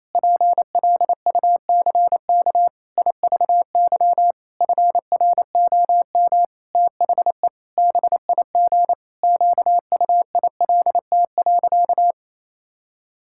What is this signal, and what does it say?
Morse code, heard as a keyed tone, about 27 wpm, 700 Hz. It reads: PLUCK IVY FROM THE BIG QUILT.
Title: Pangram Challenge v2 - 28wpm